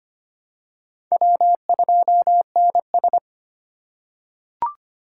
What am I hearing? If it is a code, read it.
Morse code, heard as a keyed tone, about 25 wpm, 700 Hz. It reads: W2NS
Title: USA Call Signs - 4 Characters - 3x Character Spacing - 25wpm